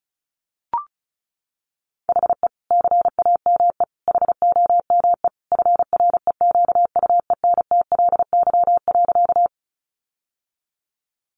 Morse code, heard as a keyed tone, about 34 wpm, 700 Hz. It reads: HE CAME HOME FREQUENTLY.
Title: Sentences from Top 1300 Words 35wpm